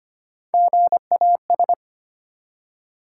Morse code, heard as a keyed tone, about 25 wpm, 700 Hz. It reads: GAS
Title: Three Letter Words - 25wpm